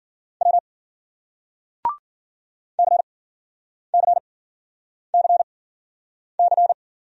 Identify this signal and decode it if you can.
Morse code, heard as a keyed tone, about 53 wpm, spaced as for 9 wpm, 700 Hz. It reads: W CCCC